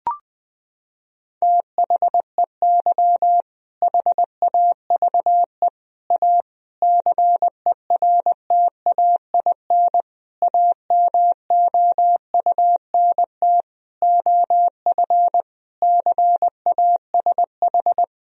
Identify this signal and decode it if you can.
Morse code, heard as a keyed tone, about 20 wpm, 700 Hz. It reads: THEY HAVE A CERTAIN AMOUNT OF CASH
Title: Sentences from Top 1000 Words - Review Part 4 - 20wpm